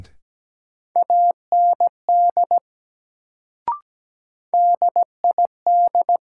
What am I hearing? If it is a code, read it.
Morse code, heard as a keyed tone, about 17 wpm, 700 Hz. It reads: AND DID
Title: Sentences from Top 100 Words - New Word Drill - 17wpm